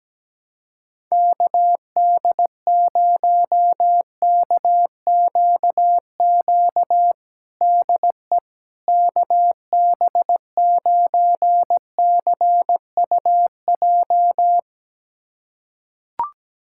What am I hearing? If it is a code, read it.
Morse code, heard as a keyed tone, about 17 wpm, 700 Hz. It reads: KD0KQQ DE KB9CUJ
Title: USA Callsign DE Callsign - 6 Characters - 17wpm